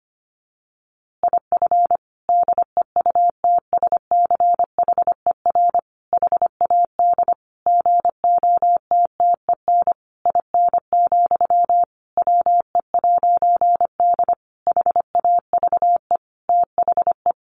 Morse code, heard as a keyed tone, about 25 wpm, 700 Hz. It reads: IF DEUTSCHER HAD GOTTEN IN, WE'D HAVE THE
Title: A Sound Of Thunder 25wpm